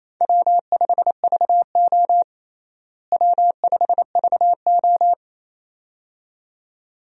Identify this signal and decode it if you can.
Morse code, heard as a keyed tone, about 27 wpm, 700 Hz. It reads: W5VO W5VO
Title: USA Call Signs - 4 Characters - Repeated 2x - 28wpm